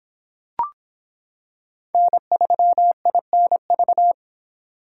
Morse code, heard as a keyed tone, about 26 wpm, 700 Hz. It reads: N3INV